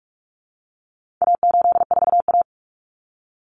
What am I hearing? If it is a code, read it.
Morse code, heard as a keyed tone, about 44 wpm, 700 Hz. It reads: A84A
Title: Common Cut Numbers - 4-Digit 45wpm